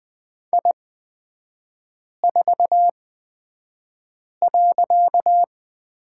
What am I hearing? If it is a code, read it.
Morse code, heard as a keyed tone, about 20 wpm, spaced as for 5 wpm, 700 Hz. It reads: I4.
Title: 255 Lesson 41e - Period - Three Characters 20wpm